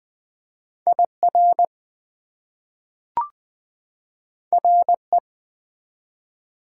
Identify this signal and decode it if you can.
Morse code, heard as a keyed tone, about 20 wpm, 700 Hz. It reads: IR RE
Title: Prefixes and Suffixes 20wpm